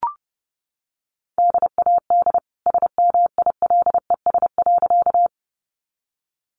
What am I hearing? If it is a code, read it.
Morse code, heard as a keyed tone, about 30 wpm, 700 Hz. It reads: DAD SMILES.
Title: Easy Phrases 30wpm